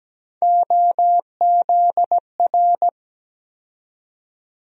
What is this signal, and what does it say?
Morse code, heard as a keyed tone, about 17 wpm, 700 Hz. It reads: OZR
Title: USA Call Sign Postfixes - 3 Letters 17wpm